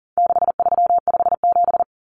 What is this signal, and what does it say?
Morse code, heard as a keyed tone, about 40 wpm, 700 Hz. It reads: 6357
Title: Common Cut Numbers - 4-Digit 40wpm